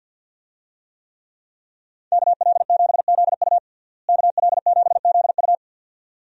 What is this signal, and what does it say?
Morse code, heard as a keyed tone, about 52 wpm, 700 Hz. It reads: KL7ZU KL7ZU
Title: USA Call Signs - 5 Characters - Repeated 2x - 50wpm